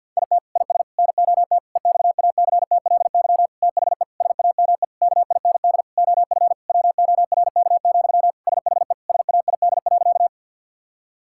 Morse code, heard as a keyed tone, about 50 wpm, 700 Hz. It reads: IT IS NOT EXACTLY THE SAME KIND OF WORK, SHE SAID.
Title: Sentences from Top 1000 Words - Review Part 6 - 50wpm